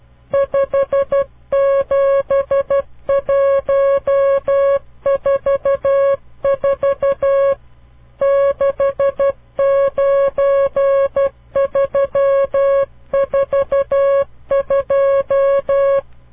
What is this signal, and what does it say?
Morse code, heard as a keyed tone, about 12 wpm, 550 Hz. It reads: 57144 69342
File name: CW Learning Programs / farnsworth / unit 5